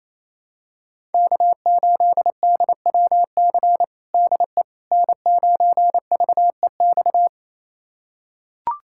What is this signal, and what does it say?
Morse code, heard as a keyed tone, about 28 wpm, 700 Hz. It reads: K8DWC DE N9VEX
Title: USA Callsign DE Callsign - 5 Characters - 28wpm